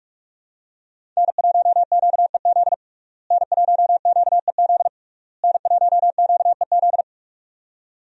Morse code, heard as a keyed tone, about 45 wpm, 700 Hz. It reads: N1QEZ N1QEZ N1QEZ
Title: USA Call Signs - 5 Characters - Repeated 3x - 45wpm